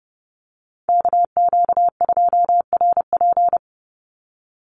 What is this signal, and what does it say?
Morse code, heard as a keyed tone, about 30 wpm, 700 Hz. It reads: KQ2RP
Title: NA SOTA Chaser Callsigns 30wpm